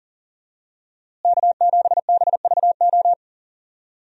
Morse code, heard as a keyed tone, about 40 wpm, 700 Hz. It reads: K7BVO
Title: USA Call Signs - 5 Characters - Repeated 2x - 40wpm